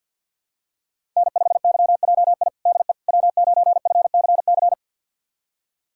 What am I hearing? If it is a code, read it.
Morse code, heard as a keyed tone, about 51 wpm, 690 Hz. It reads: N5YJI DE W9FCC
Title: USA Callsign DE Callsign - 5 Characters - 50wpm